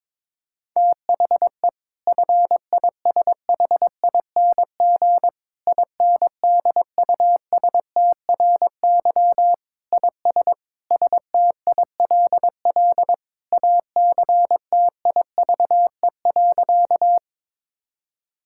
Morse code, heard as a keyed tone, about 22 wpm, 700 Hz. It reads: THE FISHING INDUSTRY IS STILL ACTIVE.
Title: Sentences from Top 1000 Words - Review Part 6 - 22wpm